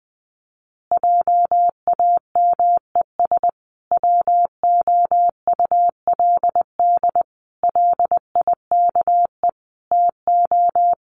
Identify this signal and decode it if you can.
Morse code, heard as a keyed tone, about 19 wpm, 700 Hz. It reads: JAMES WOULD LIKE TO
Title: Easy Phrases 20wpm